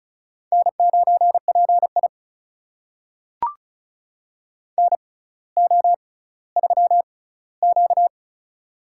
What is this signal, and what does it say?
Morse code, heard as a keyed tone, about 36 wpm, spaced as for 18 wpm, 700 Hz. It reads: N9PI NO3Q
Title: USA Call Signs - 6x Character Spacing - 35wpm